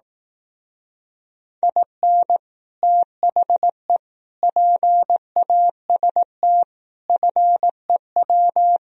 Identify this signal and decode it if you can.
Morse code, heard as a keyed tone, about 18 wpm, 700 Hz. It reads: IN THE PAST FEW
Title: Speed-Racing - Sets of 5 Words 15wpm